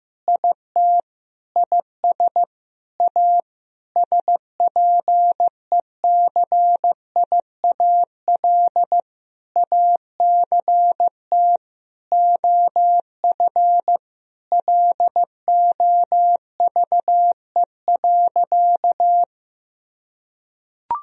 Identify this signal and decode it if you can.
Morse code, heard as a keyed tone, about 15 wpm, 700 Hz. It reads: IT IS A SPECIAL ACT OF LOVE.
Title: Sentences from Top 1000 Words - Review Part 3 - 15wpm